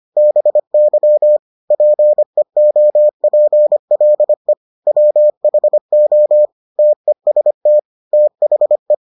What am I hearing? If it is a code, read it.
Morse code, heard as a keyed tone, about 25 wpm, 600 Hz. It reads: BY PEOPLE WHO TEST THE